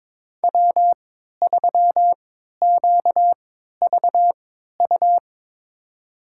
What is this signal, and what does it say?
Morse code, heard as a keyed tone, about 21 wpm, spaced as for 12 wpm, 700 Hz. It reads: W3QVU
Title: USA Call Signs - 5 Characters - 3x Character Spacing - 22wpm